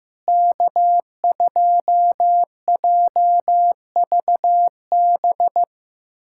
Morse code, heard as a keyed tone, about 15 wpm, 700 Hz. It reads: K2JVB
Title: USA Callsign DE Callsign - 5 Characters - 15wpm